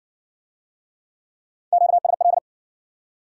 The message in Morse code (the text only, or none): QSL